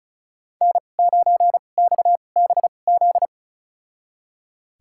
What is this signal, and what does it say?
Morse code, heard as a keyed tone, about 36 wpm, spaced as for 25 wpm, 700 Hz. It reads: N9XBZ
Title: USA Call Signs - 2x Character Spacing - 35wpm